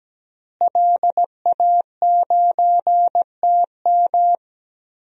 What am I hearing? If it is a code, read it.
Morse code, heard as a keyed tone, about 17 wpm, 700 Hz. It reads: LA9TM
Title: DXCC-100 Call Signs - 5 Characters - Repeated 2x - 17wpm